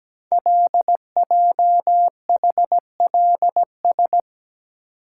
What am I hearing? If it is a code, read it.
Morse code, heard as a keyed tone, about 17 wpm, 700 Hz. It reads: LJHLS
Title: Code Groups - 5 Letters - 17wpm